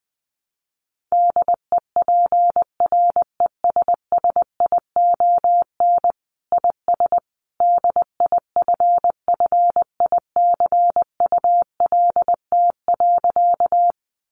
Morse code, heard as a keyed tone, about 20 wpm, 700 Hz. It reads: DEPRESSION IS DIFFICULT.
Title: Sentences from Top 1300 Words 20wpm